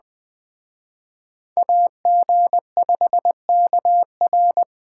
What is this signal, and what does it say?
Morse code, heard as a keyed tone, about 20 wpm, 700 Hz. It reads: AG5KR